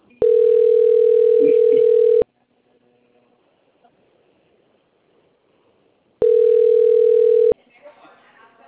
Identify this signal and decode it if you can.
Ringback tone (North America), 440 + 480 Hz, with speech.
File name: out- - - - - .30841